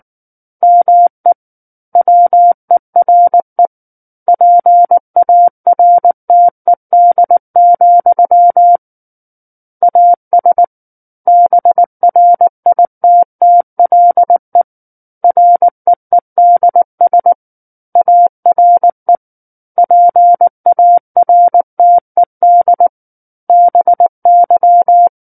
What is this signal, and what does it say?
Morse code, heard as a keyed tone, about 19 wpm, 700 Hz. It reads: ME WERE PARTED, AS BRITTLE REEDS ARE PARTED BY
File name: Book One / 10-Chapter 10 - 19 WPM